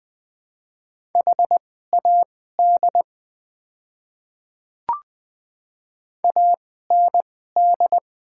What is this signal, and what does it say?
Morse code, heard as a keyed tone, about 20 wpm, spaced as for 14 wpm, 700 Hz. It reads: HAD AND
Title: 2x Character Spacing - Top 100 Words - 20wpm